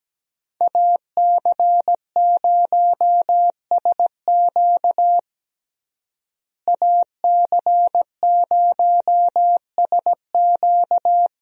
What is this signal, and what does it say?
Morse code, heard as a keyed tone, about 17 wpm, 700 Hz. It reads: AC0SQ AC0SQ
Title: USA Call Signs - 5 Characters - Repeated 3x - 17wpm